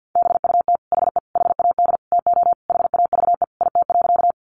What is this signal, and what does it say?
Morse code, heard as a keyed tone, about 50 wpm, 700 Hz. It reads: BUT HE HAD TO HAVE IT.
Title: Sentences from Top 100 Words 50wpm